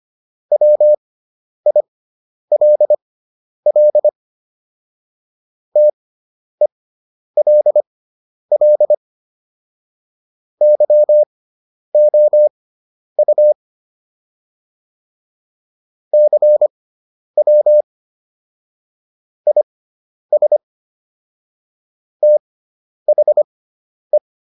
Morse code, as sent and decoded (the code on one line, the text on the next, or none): .-- .. .-.. .-.. / - . .-.. .-.. / -.-- --- ..- / -.-. .-- / .. ... / - .... .
WILL TELL YOU CW IS THE